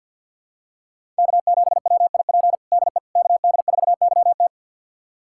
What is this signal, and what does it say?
Morse code, heard as a keyed tone, about 52 wpm, 700 Hz. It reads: K7WIP DE KD4YT
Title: USA Callsign DE Callsign - 5 Characters - 50wpm